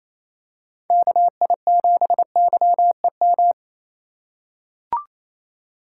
Morse code, heard as a keyed tone, about 28 wpm, 700 Hz. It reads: KI7YEM
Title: USA Call Signs - 6 Characters - 3x Character Spacing - 28wpm